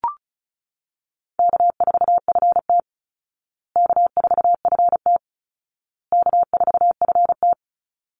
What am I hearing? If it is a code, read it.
Morse code, heard as a keyed tone, about 34 wpm, 700 Hz. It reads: K4FT K4FT K4FT
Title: Call Signs Repeated 2x - 35wpm